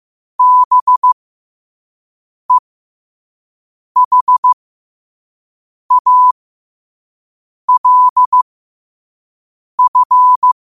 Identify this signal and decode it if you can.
Morse code, heard as a keyed tone, about 15 wpm, spaced as for 5 wpm, 1000 Hz. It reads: BEHALF
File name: qotdfiles / 5wpm